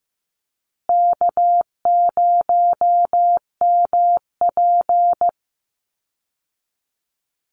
Morse code, heard as a keyed tone, about 15 wpm, 700 Hz. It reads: K0MP
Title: Call Signs Repeated 2x - 15wpm